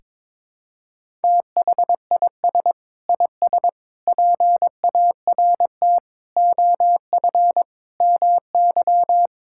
Morse code, heard as a keyed tone, about 22 wpm, 700 Hz. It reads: THIS IS PART OF MY
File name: practice / 22wpm / Sentences with 2-Word Phrases - 22wpm